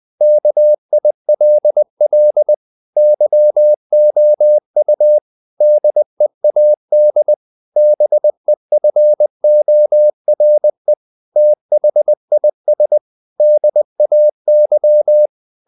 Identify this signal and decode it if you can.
Morse code, heard as a keyed tone, about 20 wpm, 600 Hz. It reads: KILL YOU DEAD BEFORE THIS DAY
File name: the old man and the sea / the old man and the sea part 1 20wpm